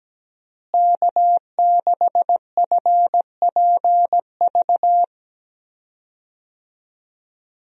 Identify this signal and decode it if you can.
Morse code, heard as a keyed tone, about 17 wpm, 700 Hz. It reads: K6FPV
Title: USA Call Signs - 5 Characters - Repeated 3x - 17wpm